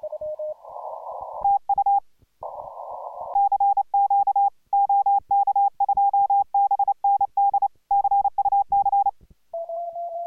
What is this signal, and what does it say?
Morse code, heard as a keyed tone, about 28 wpm, 810 Hz. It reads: TU CQ OK2BND CUC